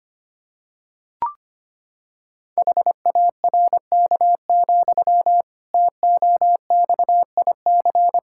Morse code, heard as a keyed tone, about 25 wpm, 700 Hz. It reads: HARK, TOXIC
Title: Pangram Challenge v2 - 25wpm